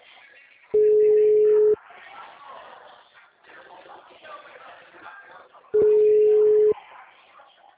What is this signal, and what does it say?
Ringback tone (Europe, 425 Hz), with speech.
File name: out- -2006- - - .172